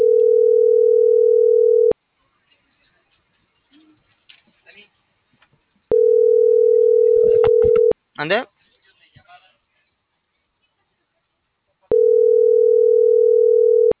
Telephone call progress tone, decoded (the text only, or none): ringback tone